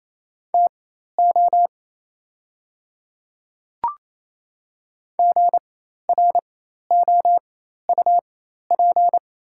Morse code, heard as a keyed tone, about 28 wpm, spaced as for 13 wpm, 690 Hz. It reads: TO GROUP
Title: 4x Character Spacing - Top 300 Words - 28wpm